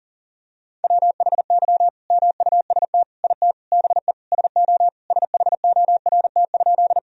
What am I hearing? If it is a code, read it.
Morse code, heard as a keyed tone, about 37 wpm, 700 Hz. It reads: WHY MUST IT BE SO SHORT?
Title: Sentences from Top 1000 Words - Review Part 2 - 40wpm